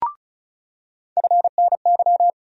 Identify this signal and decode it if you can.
Morse code, heard as a keyed tone, about 36 wpm, 700 Hz. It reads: FNY
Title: USA Call Sign Postfixes - 3 Letters 35wpm